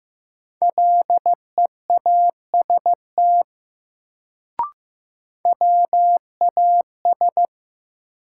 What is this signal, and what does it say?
Morse code, heard as a keyed tone, about 16 wpm, 700 Hz. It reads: LEAST WAS
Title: Sets of 2 Words - New Word Drill - 15wpm